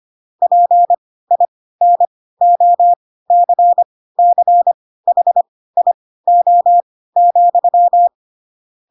Morse code, heard as a keyed tone, about 25 wpm, spaced as for 16 wpm, 700 Hz. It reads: PINOCCHIO,
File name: Pinocchio / Pinocchio ch4 10 25 ik1hge